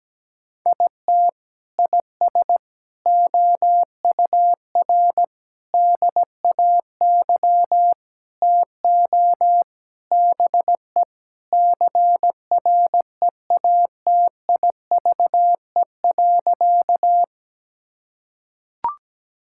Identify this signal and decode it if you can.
Morse code, heard as a keyed tone, about 17 wpm, 700 Hz. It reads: IT IS OUR DAY TO BE CREATIVE.
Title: Sentences from Top 1000 Words - Review Part 5 - 17wpm